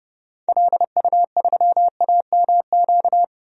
Morse code, heard as a keyed tone, about 30 wpm, 700 Hz. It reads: LU3AMQ
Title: DXCC-100 Call Signs - 6 Characters - 30wpm